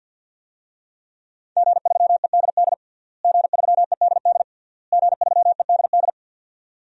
Morse code, heard as a keyed tone, about 52 wpm, 700 Hz. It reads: G3EDD G3EDD G3EDD
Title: DXCC-100 Call Signs - 5 Characters - Repeated 3x - 50wpm